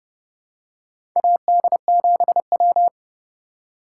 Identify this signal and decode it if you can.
Morse code, heard as a keyed tone, about 32 wpm, 700 Hz. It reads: AD7W
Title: USA Call Signs - 4 Characters - 3x Character Spacing - 30wpm